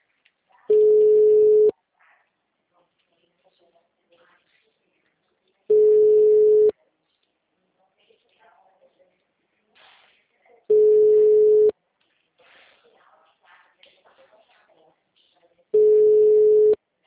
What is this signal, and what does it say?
Ringback tone (Europe, 425 Hz), with speech.